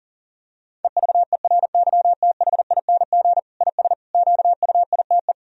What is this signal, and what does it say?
Morse code, heard as a keyed tone, about 40 wpm, 700 Hz. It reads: EVERYTHING IS QUITE
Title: Sentences from Top 1000 Words - Review Part 2 - 40wpm